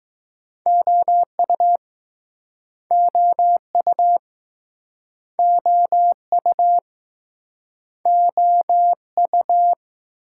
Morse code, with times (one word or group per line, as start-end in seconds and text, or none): 0.7-1.8 OU
2.9-4.2 OU
5.4-6.8 OU
8.1-9.7 OU